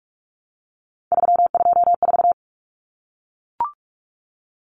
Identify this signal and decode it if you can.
Morse code, heard as a keyed tone, about 44 wpm, 700 Hz. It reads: 324
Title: Common Cut Numbers - 3-Digit 45wpm